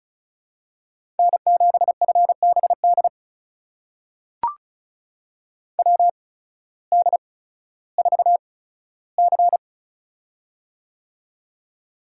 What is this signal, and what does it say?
Morse code, heard as a keyed tone, about 36 wpm, spaced as for 16 wpm, 700 Hz. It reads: N7FBD WD4C
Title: USA Call Signs - 8x Character Spacing - 35wpm